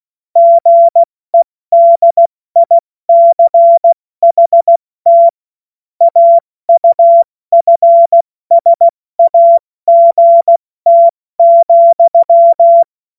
Morse code, heard as a keyed tone, about 16 wpm, spaced as for 14 wpm, 680 Hz. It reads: GEDICHT AUFSAGT,
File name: Morsekurs / Witze / Witz 09. November 2021, 70 F80